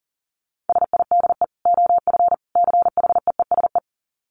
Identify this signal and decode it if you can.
Morse code, heard as a keyed tone, about 40 wpm, 700 Hz. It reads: SIDE OF CHEESE
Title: Dit Challenge 40wpm